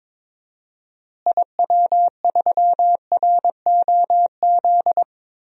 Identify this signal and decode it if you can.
Morse code, heard as a keyed tone, about 22 wpm, 700 Hz. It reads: IW3ROZ
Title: DXCC-100 Call Signs - 6 Characters - Repeated 2x - 22wpm